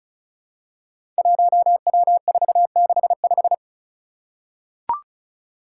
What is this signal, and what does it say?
Morse code, heard as a keyed tone, about 36 wpm, 700 Hz. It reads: 1W465
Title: Five Challenging Letter-Number Combinations 35wpm